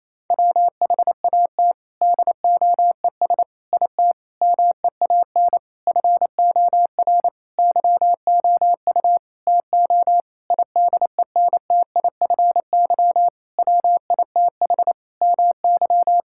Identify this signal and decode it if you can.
Morse code, heard as a keyed tone, about 28 wpm, 700 Hz. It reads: WHAT DOES IT MEAN FOR YOU TO IDENTIFY WITH MY
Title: Sentences from Top 1000 Words - Review Part 6 - 28wpm